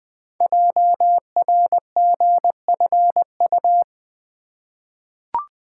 Morse code, heard as a keyed tone, about 20 wpm, 700 Hz. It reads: JRGFU
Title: Code Groups - 5 Letters - 20wpm